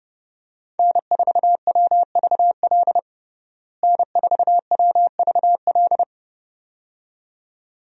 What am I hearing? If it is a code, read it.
Morse code, heard as a keyed tone, about 30 wpm, 700 Hz. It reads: N4WVL N4WVL
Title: USA Call Signs - 5 Characters - Repeated 2x - 30wpm